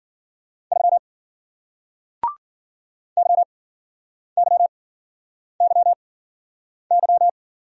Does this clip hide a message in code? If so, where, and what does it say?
Yes: Morse code from 0.7 to 7.3 s, heard as a keyed tone, about 55 wpm, 700 Hz. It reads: 3 Y Y Y Y